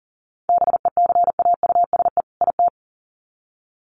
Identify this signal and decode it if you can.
Morse code, heard as a keyed tone, about 40 wpm, 700 Hz. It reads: BECAUSE IT
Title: Sets of 2 Words 40wpm